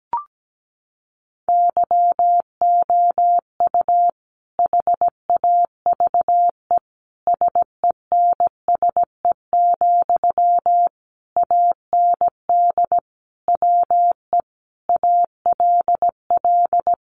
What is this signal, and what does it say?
Morse code, heard as a keyed tone, about 17 wpm, 700 Hz. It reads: YOU HAVE SENSE, AND WE ALL
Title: Sentences from Top 1100 Words 17wpm